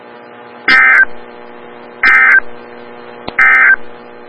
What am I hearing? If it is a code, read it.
Emergency Alert System (SAME) tones, heard as harsh buzzing bursts. It is heard from 0.6 to 3.8 s.
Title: Des Moines, IA NOAA Weather Radio - WXL57